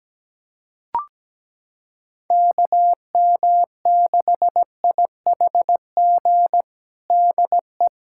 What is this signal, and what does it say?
Morse code, heard as a keyed tone, about 17 wpm, 700 Hz. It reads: KM6IHG DE
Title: USA Callsign DE Callsign - 6 Characters - 17wpm